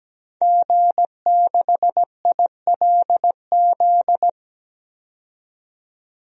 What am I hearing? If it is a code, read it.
Morse code, heard as a keyed tone, about 17 wpm, 700 Hz. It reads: G6ILZ
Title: DXCC-100 Call Signs - 5 Characters - 17wpm